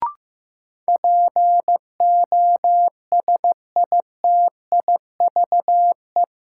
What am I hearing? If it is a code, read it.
Morse code, heard as a keyed tone, about 15 wpm, 700 Hz. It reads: POSITIVE
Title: Sentences from Top 1000 Words - Review Part 5 - New Word Drill - 15wpm